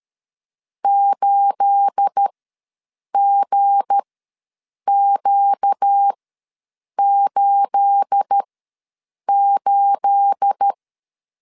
Morse code, heard as a keyed tone, about 13 wpm, spaced as for 7 wpm, 800 Hz. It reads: 8GQ88
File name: LEZIONI / 9 28 ws 12wpm 3354